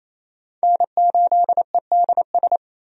Morse code, heard as a keyed tone, about 27 wpm, 700 Hz. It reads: N8EDS
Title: USA Call Signs - 5 Characters - 28wpm